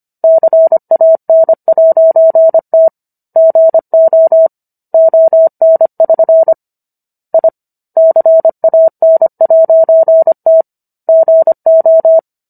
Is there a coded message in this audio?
Yes: Morse code from 0.2 to 12.2 s, heard as a keyed tone, about 25 wpm, 650 Hz. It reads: CAN'T GO ON<SN> I CAN'T GO